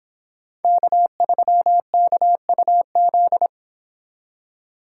Morse code, heard as a keyed tone, about 26 wpm, 700 Hz. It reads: K3KUZ